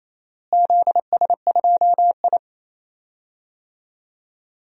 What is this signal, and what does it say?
Morse code, heard as a keyed tone, about 28 wpm, 700 Hz. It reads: ZS2I
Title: DXCC-100 Call Signs - 4 Characters - Repeated 2x - 28wpm